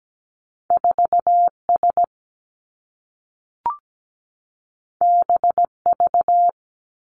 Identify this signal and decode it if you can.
Morse code, heard as a keyed tone, about 17 wpm, 700 Hz. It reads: 4 S B V
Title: Two Challenging Letter-Number Combinations 17wpm